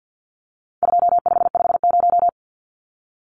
Morse code, heard as a keyed tone, about 50 wpm, 700 Hz. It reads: 2550
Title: Numbers - 4-Digit 50wpm